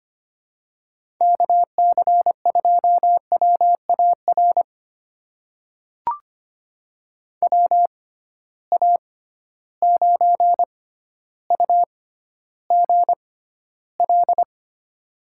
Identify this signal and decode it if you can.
Morse code, heard as a keyed tone, about 25 wpm, spaced as for 13 wpm, 700 Hz. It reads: KC2WAR WA9UGL
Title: USA Call Signs - 6 Characters - 6x Character Spacing - 25wpm